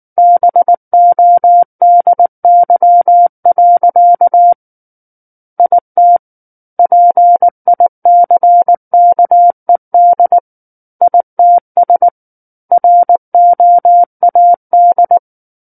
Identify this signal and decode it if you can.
Morse code, heard as a keyed tone, about 19 wpm, 700 Hz. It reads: BODY. IT PICKED ITS ROAD
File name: Book One / 10-Chapter 10 - 19 WPM